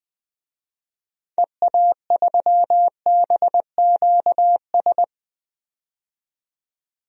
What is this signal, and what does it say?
Morse code, heard as a keyed tone, about 20 wpm, 700 Hz. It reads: EA3BQS
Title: DXCC-100 Call Signs - 6 Characters - 20wpm